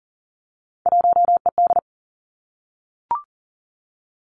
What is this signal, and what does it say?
Morse code, heard as a keyed tone, about 42 wpm, 700 Hz. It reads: 1ED
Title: Three Challenging Letter-Number Combinations 40wpm